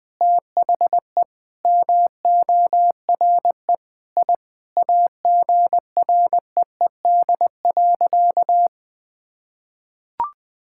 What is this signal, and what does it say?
Morse code, heard as a keyed tone, about 20 wpm, 700 Hz. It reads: THE MORE I AGREED.
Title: Sentences with 9-Word Phrases - 20wpm